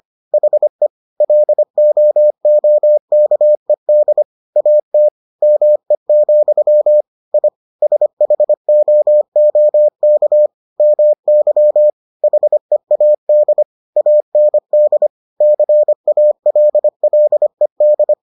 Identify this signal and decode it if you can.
Morse code, heard as a keyed tone, about 25 wpm, 600 Hz. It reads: HE LOOKED AT ME, I SHOOK MY HEAD AND CALLED